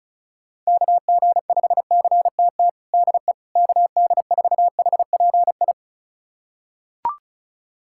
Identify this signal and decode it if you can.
Morse code, heard as a keyed tone, about 36 wpm, 700 Hz. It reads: KG5CTT DE KD4HPI
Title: USA Callsign DE Callsign 35wpm